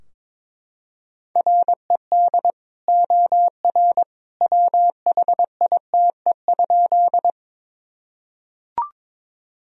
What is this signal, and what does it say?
Morse code, heard as a keyed tone, about 22 wpm, 700 Hz. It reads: RED OR WHITE?
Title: Sentences from Top 1000 Words - Review Part 2 - 22wpm